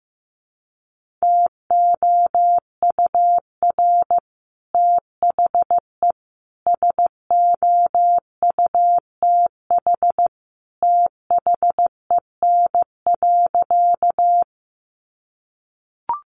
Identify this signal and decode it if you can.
Morse code, heard as a keyed tone, about 15 wpm, 700 Hz. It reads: TOUR THE SOUTH THEN.
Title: Sentences from Top 1400 Words 15wpm